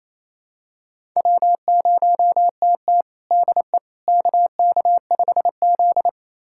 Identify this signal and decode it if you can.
Morse code, heard as a keyed tone, about 28 wpm, 700 Hz. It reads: W0TT DE KK5Z
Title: USA Callsign DE Callsign - 4 Characters - 28wpm